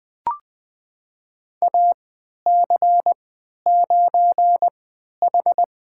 Morse code, heard as a keyed tone, about 20 wpm, spaced as for 11 wpm, 700 Hz. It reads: AC9H
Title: USA Call Signs - 3x Character Spacing - 20wpm